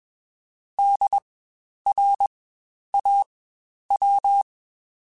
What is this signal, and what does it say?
Morse code, heard as a keyed tone, about 21 wpm, spaced as for 10 wpm, 800 Hz. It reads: DRAW
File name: Spaced Alphas / Spaced Alphas mp3s / Ds sp @20